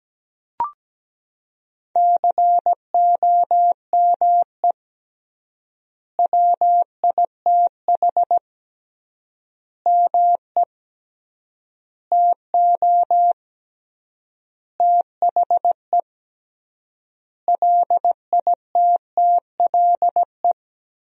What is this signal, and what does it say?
Morse code, heard as a keyed tone, about 17 wpm, 700 Hz. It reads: COME WITH ME TO THE LITTLE
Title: 3x Word-Spacing - Sentences from Top 200 Words - 17wpm